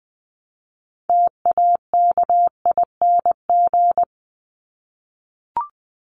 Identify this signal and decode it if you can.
Morse code, heard as a keyed tone, about 20 wpm, 700 Hz. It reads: TAKING
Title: ING Suffix - Top 100 Words with Base Words 20wpm